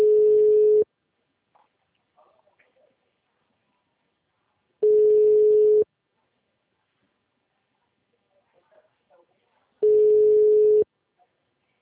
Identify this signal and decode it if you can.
Ringback tone (Europe, 425 Hz), with speech.